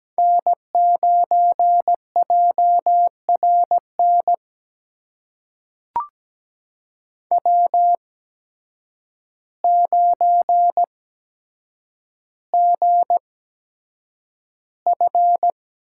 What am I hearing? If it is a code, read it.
Morse code, heard as a keyed tone, about 17 wpm, spaced as for 8 wpm, 700 Hz. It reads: N9JRN W9GF
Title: USA Call Signs - 5 Characters - 8x Character Spacing - 17wpm